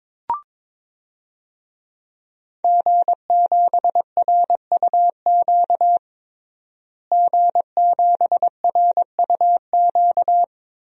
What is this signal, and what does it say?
Morse code, heard as a keyed tone, about 22 wpm, 700 Hz. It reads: G7RUQ G7RUQ
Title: DXCC-100 Call Signs - 5 Characters - Repeated 2x - 22wpm